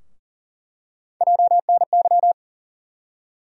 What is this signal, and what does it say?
Morse code, heard as a keyed tone, about 40 wpm, 700 Hz. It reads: JNY
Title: USA Call Sign Postfixes - 3 Letters 40wpm